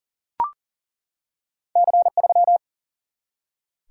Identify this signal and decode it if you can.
Morse code, heard as a keyed tone, about 41 wpm, 700 Hz. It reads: C3
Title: Single-Letter Single-Number 40wpm